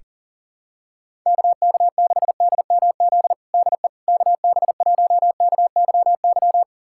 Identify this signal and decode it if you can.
Morse code, heard as a keyed tone, about 40 wpm, 700 Hz. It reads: KK6DMZ DE KB1KYY
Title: USA Callsign DE Callsign - 6 Characters - 40wpm